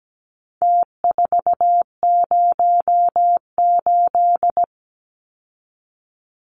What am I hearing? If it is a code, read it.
Morse code, heard as a keyed tone, about 17 wpm, 700 Hz. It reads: T408
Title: Numbers - Contest Style - 4-Digit 17wpm